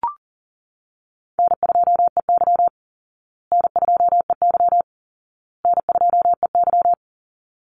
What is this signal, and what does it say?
Morse code, heard as a keyed tone, about 40 wpm, 700 Hz. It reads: N2EY N2EY N2EY
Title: Call Signs Repeated 2x - 40wpm